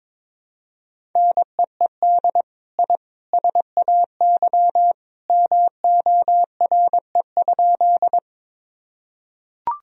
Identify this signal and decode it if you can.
Morse code, heard as a keyed tone, about 21 wpm, 700 Hz. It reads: NEED I SAY MORE?
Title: Sentences from Top 1000 Words - Review Part 1 - 22wpm